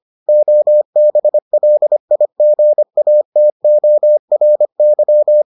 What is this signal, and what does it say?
Morse code, heard as a keyed tone, about 25 wpm, 600 Hz. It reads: OBLIGATORY